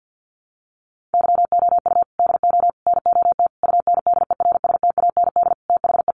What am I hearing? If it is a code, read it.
Morse code, heard as a keyed tone, about 50 wpm, 700 Hz. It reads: YOU DO NOT UNDERSTAND THE
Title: Sentences from Top 600 Words 50wpm